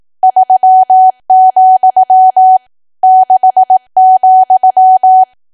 Morse code, heard as a keyed tone, about 18 wpm, 750 Hz. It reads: 3, 6,
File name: cwa / W1AW-Level2 / 18WPM